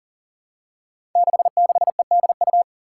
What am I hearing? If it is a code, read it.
Morse code, heard as a keyed tone, about 42 wpm, 700 Hz. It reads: 66EDU
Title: Five Challenging Letter-Number Combinations 40wpm